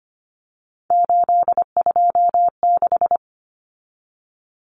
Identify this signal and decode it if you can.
Morse code, heard as a keyed tone, about 25 wpm, 700 Hz. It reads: 826
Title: Common Cut Numbers - 3-Digit 25wpm